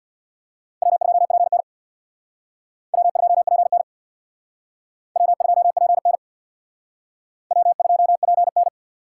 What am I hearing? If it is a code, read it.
Morse code, heard as a keyed tone, about 64 wpm, 700 Hz. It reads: W2PN W2PN W2PN W2PN